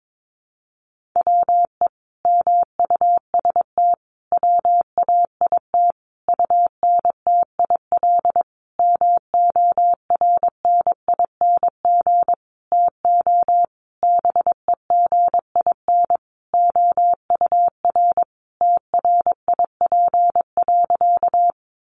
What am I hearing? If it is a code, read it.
Morse code, heard as a keyed tone, about 22 wpm, 700 Hz. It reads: WE MUST WAIT UNTIL MORNING TO BEGIN OUR TRIP.
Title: Sentences from Top 1300 Words 22wpm